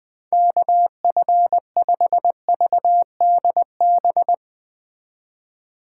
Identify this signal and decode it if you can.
Morse code, heard as a keyed tone, about 20 wpm, 700 Hz. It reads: KF5VDB
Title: Speed Racing - USA Call Signs 17wpm